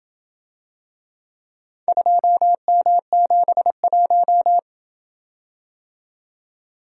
Morse code, heard as a keyed tone, about 27 wpm, 700 Hz. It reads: 2M71